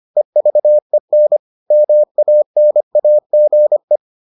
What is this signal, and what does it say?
Morse code, heard as a keyed tone, about 25 wpm, 600 Hz. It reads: EVEN MANAGE